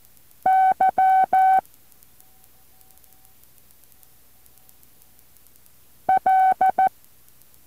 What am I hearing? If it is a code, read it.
Morse code, heard as a keyed tone, about 14 wpm, 760 Hz. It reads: Y L